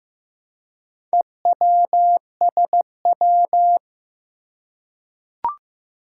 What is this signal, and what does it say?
Morse code, heard as a keyed tone, about 15 wpm, 700 Hz. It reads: EWSW
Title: Four Challenging Letter-Number Combinations 15wpm